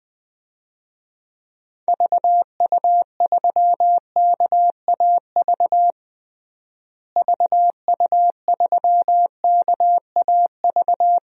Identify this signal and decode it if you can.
Morse code, heard as a keyed tone, about 20 wpm, 700 Hz. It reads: VU3KAV VU3KAV
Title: DXCC-100 Call Signs - 6 Characters - Repeated 3x - 20wpm